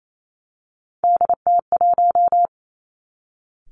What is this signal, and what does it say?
Morse code, heard as a keyed tone, about 28 wpm, 700 Hz. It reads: DT1
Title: Three Challenging Letter-Number Combinations 28wpm